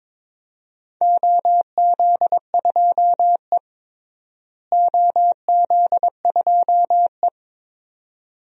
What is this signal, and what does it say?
Morse code, heard as a keyed tone, about 22 wpm, 700 Hz. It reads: OZ2E OZ2E
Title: DXCC-100 Call Signs - 4 Characters - Repeated 2x - 22wpm